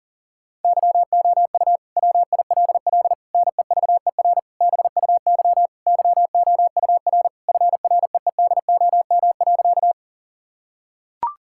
Morse code, heard as a keyed tone, about 40 wpm, 700 Hz. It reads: YOU WILL NEVER BUY YOUR FREEDOM.
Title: Sentences from Top 1000 Words - Review Part 5 - 40wpm